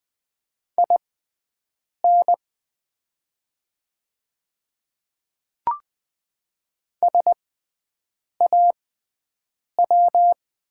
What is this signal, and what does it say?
Morse code, heard as a keyed tone, about 20 wpm, spaced as for 7 wpm, 690 Hz. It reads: IN SAW
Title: 6x Character Spacing - Top 300 Words - 20wpm